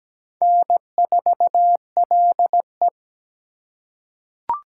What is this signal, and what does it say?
Morse code, heard as a keyed tone, about 17 wpm, 700 Hz. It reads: N4LE
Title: USA Call Signs - 4 Characters - 8x Character Spacing - 17wpm